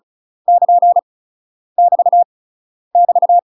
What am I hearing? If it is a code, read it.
Morse code, heard as a keyed tone, about 36 wpm, 700 Hz. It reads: <KN> <BT> <BT>
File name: UK MASTER QSOs TRAINING / 35WPM / UK MASTER QSOs 0028